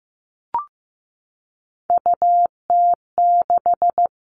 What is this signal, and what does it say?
Morse code, heard as a keyed tone, about 15 wpm, 700 Hz. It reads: UT6
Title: All Cut Numbers - 3-Digit 15wpm